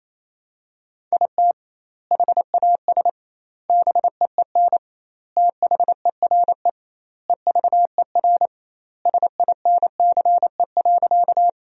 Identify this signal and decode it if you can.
Morse code, heard as a keyed tone, about 28 wpm, 700 Hz. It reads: IT HAS BEEN THERE EVER SINCE.
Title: 2x Word-Spacing - Sentences from Top 400 Words - 28wpm